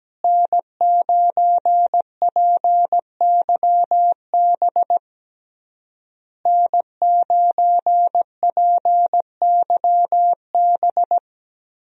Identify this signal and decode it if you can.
Morse code, heard as a keyed tone, about 17 wpm, 700 Hz. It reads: N9PYB N9PYB
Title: USA Call Signs - 5 Characters - Repeated 3x - 17wpm